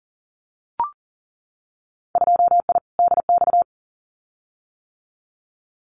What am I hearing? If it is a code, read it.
Morse code, heard as a keyed tone, about 40 wpm, 700 Hz. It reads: 2I DX
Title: Winter Field Day Exchanges 40wpm